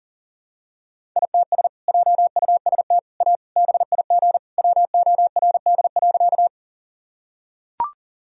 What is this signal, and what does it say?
Morse code, heard as a keyed tone, about 40 wpm, 700 Hz. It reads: ITS JUST A BIG WORD.
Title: Sentences from Top 1000 Words - Review Part 1 - 40wpm